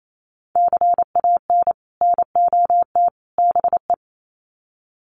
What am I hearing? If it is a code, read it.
Morse code, heard as a keyed tone, about 28 wpm, 700 Hz. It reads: CAN NOT BE
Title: Sets of 3 Words 28wpm